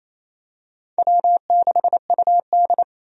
Morse code, heard as a keyed tone, about 28 wpm, 700 Hz. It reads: W6UD
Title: USA Call Signs - 4 Characters - 28wpm